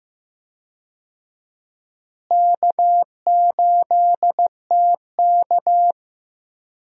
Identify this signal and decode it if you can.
Morse code, heard as a keyed tone, about 15 wpm, 700 Hz. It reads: K8TK
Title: USA Call Signs - 4 Characters - Repeated 3x - 15wpm